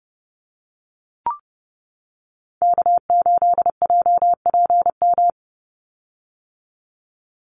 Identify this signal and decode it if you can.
Morse code, heard as a keyed tone, about 30 wpm, 700 Hz. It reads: K8JPM
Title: Call Signs - 30wpm